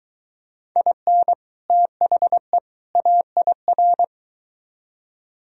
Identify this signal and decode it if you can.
Morse code, heard as a keyed tone, about 23 wpm, 700 Hz. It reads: IN THE AIR